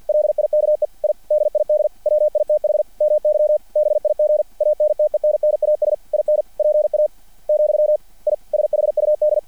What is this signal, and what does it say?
Morse code, heard as a keyed tone, about 50 wpm, 600 Hz. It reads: LIKE I DID WITH MY BIG ANTENNAS IN PA , I SHUD
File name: archived / CRSnet